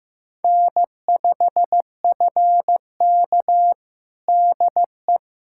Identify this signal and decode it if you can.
Morse code, heard as a keyed tone, about 15 wpm, 700 Hz. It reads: N5FK DE
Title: USA Callsign DE Callsign - 4 Characters - 15wpm